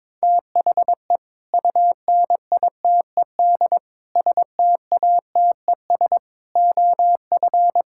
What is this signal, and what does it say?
Morse code, heard as a keyed tone, about 21 wpm, 700 Hz. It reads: THE UNITED STATES OF
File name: practice / 22wpm / Build A Phrase - 5-Word Phrases - 22wpm